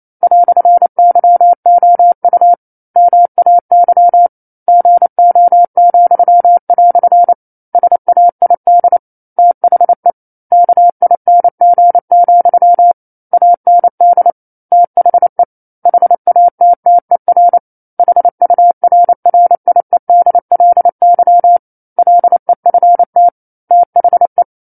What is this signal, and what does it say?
Morse code, heard as a keyed tone, about 28 wpm, 700 Hz. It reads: "YOU MAY GO," SAID THE KING, AND THE HATTER HURRIEDLY LEFT THE
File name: 09-Chapter-09 28 WPM